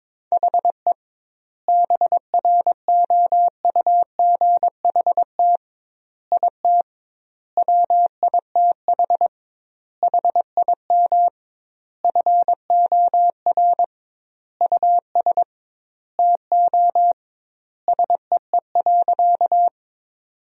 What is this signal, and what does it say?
Morse code, heard as a keyed tone, about 22 wpm, 700 Hz. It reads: HE BROUGHT IT WITH HIM FOR US TO SEE.
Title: 2x Word-Spacing - Sentences from Top 500 Words - 22wpm